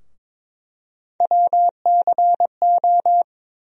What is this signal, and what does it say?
Morse code, heard as a keyed tone, about 22 wpm, 700 Hz. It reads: WCO
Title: USA Call Sign Postfixes - 3 Letters 22wpm